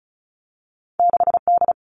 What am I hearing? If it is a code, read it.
Morse code, heard as a keyed tone, about 36 wpm, 700 Hz. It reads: 6D
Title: All Cut Numbers - 2-Digit 35wpm